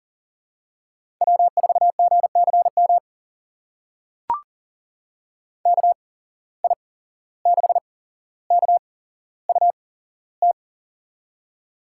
Morse code, heard as a keyed tone, about 40 wpm, spaced as for 16 wpm, 700 Hz. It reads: W4GCM KI6KUT
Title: USA Call Signs - 8x Character Spacing - 40wpm